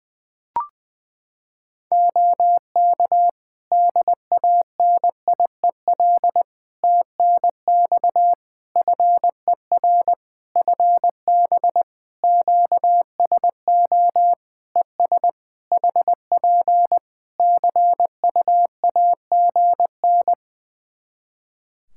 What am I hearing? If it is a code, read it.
Morse code, heard as a keyed tone, about 20 wpm, 700 Hz. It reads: OK DANIEL TNX FER FB QSO ES HP CUAGN
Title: Practice QSO - Exchange 4 - Ending - 20wpm